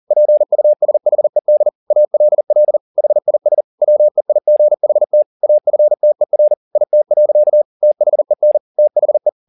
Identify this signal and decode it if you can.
Morse code, heard as a keyed tone, about 40 wpm, 600 Hz. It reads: PUSHED ALL HIS WEIGHT AFTER IT. THEN THE
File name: the old man and the sea 2 40wpm